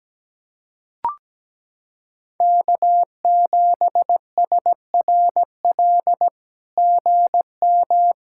Morse code, heard as a keyed tone, about 17 wpm, 700 Hz. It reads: K7SRL GM
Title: SOTA QSO Practice 17wpm